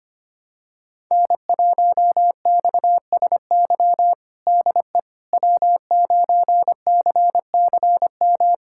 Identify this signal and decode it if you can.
Morse code, heard as a keyed tone, about 25 wpm, 700 Hz. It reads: N1XSY DE W9CCM
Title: USA Callsign DE Callsign - 5 Characters - 25wpm